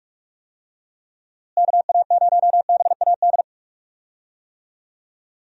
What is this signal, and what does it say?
Morse code, heard as a keyed tone, about 41 wpm, 700 Hz. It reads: KA0BAD
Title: USA Call Signs - 6 Characters - Repeated 2x - 45wpm